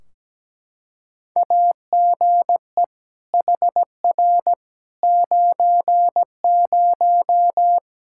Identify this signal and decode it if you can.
Morse code, heard as a keyed tone, about 17 wpm, 700 Hz. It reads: AGE HR 90
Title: Practice QSO - Exchange 3 - More Info - 17wpm